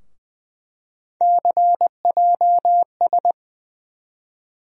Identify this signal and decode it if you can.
Morse code, heard as a keyed tone, about 20 wpm, 700 Hz. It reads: CJS
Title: USA Call Sign Postfixes - 3 Letters 20wpm